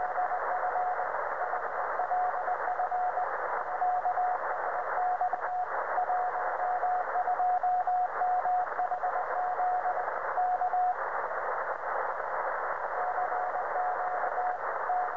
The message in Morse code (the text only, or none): none